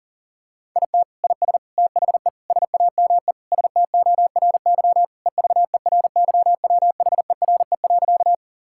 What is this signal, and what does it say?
Morse code, heard as a keyed tone, about 40 wpm, 700 Hz. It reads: IT IS THE SAME STORY EVERYWHERE.
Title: Sentences from Top 1600 Words 40wpm